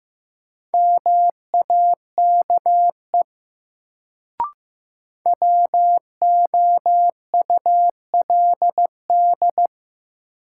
Morse code, heard as a keyed tone, about 16 wpm, 700 Hz. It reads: MAKE WOULD
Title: Sentences from Top 100 Words - New Word Drill - 15wpm